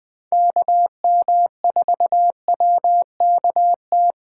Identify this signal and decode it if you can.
Morse code, heard as a keyed tone, about 20 wpm, 700 Hz. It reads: KM4WKT